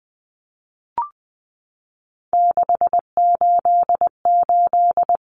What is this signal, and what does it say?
Morse code, heard as a keyed tone, about 20 wpm, 700 Hz. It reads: 688
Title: Numbers - 3-Digit 20wpm